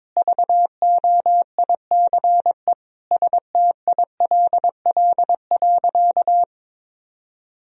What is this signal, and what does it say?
Morse code, heard as a keyed tone, about 22 wpm, 700 Hz. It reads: VOICE STILL.
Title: Sentences from Top 500 Words and Abbreviations 22wpm